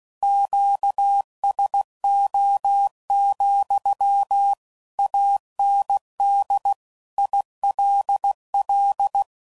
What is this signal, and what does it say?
Morse code, heard as a keyed tone, about 16 wpm, 800 Hz. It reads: QSO, AND ILL
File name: CW Code Groups @15WPM / QSOs / kc7haa